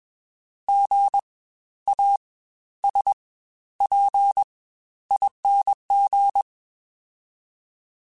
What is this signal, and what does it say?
Morse code, heard as a keyed tone, about 21 wpm, spaced as for 12 wpm, 800 Hz. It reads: GASPING
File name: Spaced Alphas / Spaced Alphas mp3s / Gs sp @20